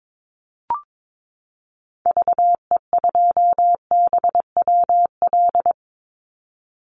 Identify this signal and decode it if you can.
Morse code, heard as a keyed tone, about 22 wpm, 700 Hz. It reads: VE2BWL
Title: Call Signs - 22wpm